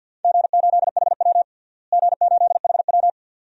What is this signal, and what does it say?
Morse code, heard as a keyed tone, about 46 wpm, 700 Hz. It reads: G8HW G8HW
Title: DXCC-100 Call Signs - 4 Characters - Repeated 2x - 50wpm